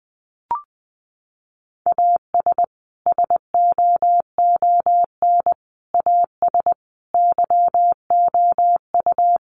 Morse code, heard as a keyed tone, about 20 wpm, 700 Hz. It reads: AS SOON AS YOU
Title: Sentences from Top 1500 Words 20wpm